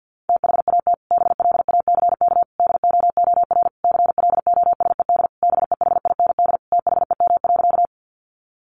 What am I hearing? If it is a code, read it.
Morse code, heard as a keyed tone, about 48 wpm, 700 Hz. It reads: THAT BLACK DOOR CLOSED BEHIND THEM.
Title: Sentences from Top 1400 Words 50wpm